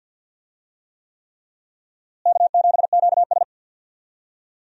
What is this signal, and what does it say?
Morse code, heard as a keyed tone, about 51 wpm, 700 Hz. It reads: K7QS
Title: USA Call Signs - 4 Characters - Repeated 3x - 50wpm